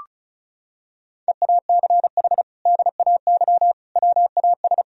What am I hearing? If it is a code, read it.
Morse code, heard as a keyed tone, about 34 wpm, 700 Hz. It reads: EACH DAY WAS
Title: Sentences from Top 1000 Words - Review Part 1 - 35wpm